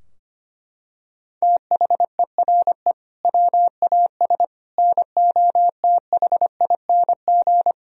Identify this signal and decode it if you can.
Morse code, heard as a keyed tone, about 25 wpm, 700 Hz. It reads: THERE WAS NOTHING
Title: Sentences from Top 1000 Words - Review Part 4 - 25wpm